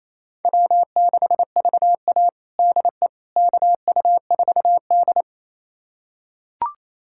Morse code, heard as a keyed tone, about 28 wpm, 700 Hz. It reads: W6VA DE KU4D
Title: USA Callsign DE Callsign - 4 Characters - 28wpm